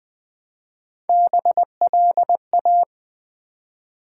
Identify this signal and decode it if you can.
Morse code, heard as a keyed tone, about 20 wpm, 700 Hz. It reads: BLA
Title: Build A Word - 1 Letter At A Time - 5-Letter Words - 20wpm